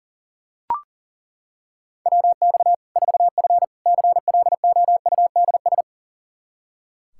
Morse code, heard as a keyed tone, about 40 wpm, 700 Hz. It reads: WX 4F CLOUDS
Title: Practice QSO - Exchange 2 - Info - 40wpm